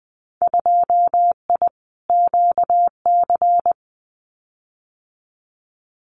Morse code, heard as a keyed tone, about 20 wpm, 700 Hz. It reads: 2I QC
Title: Winter Field Day Exchanges 20wpm